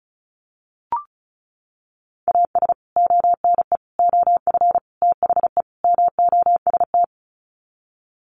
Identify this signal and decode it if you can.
Morse code, heard as a keyed tone, about 34 wpm, 700 Hz. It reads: AS ONE OF THE MOST
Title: Sets of 5 Words 35wpm